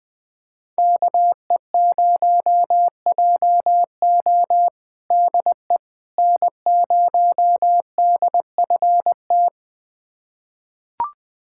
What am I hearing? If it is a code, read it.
Morse code, heard as a keyed tone, about 20 wpm, 700 Hz. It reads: KE0JO DE N0DFT
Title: USA Callsign DE Callsign - 5 Characters - 20wpm